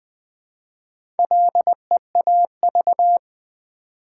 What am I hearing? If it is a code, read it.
Morse code, heard as a keyed tone, about 20 wpm, 700 Hz. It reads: LEAV
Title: Build A Word - 1 Letter At A Time - 5-Letter Words - 20wpm